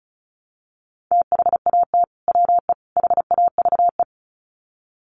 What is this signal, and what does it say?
Morse code, heard as a keyed tone, about 34 wpm, 700 Hz. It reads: THAT WE HAVE
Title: Sets of 3 Words 35wpm